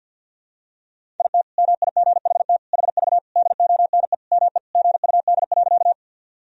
Morse code, heard as a keyed tone, about 47 wpm, 700 Hz. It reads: IT MIGHT HV DONE ME GUD.
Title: Sentences from Top 500 Words and Abbreviations 50wpm